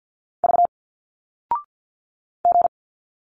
Morse code, heard as a keyed tone, about 51 wpm, 700 Hz. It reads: V G
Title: Single Letters 50wpm